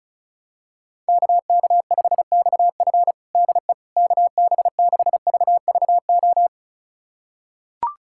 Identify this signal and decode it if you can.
Morse code, heard as a keyed tone, about 36 wpm, 700 Hz. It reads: KK5XF DE KB6VVO
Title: USA Callsign DE Callsign 35wpm